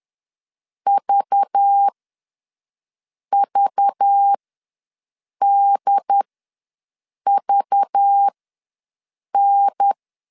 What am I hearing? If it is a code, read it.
Morse code, heard as a keyed tone, about 10 wpm, spaced as for 6 wpm, 800 Hz. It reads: VVDVN
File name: LEZIONI / 5 LBVC / 5 LBV ws 10wpm 3645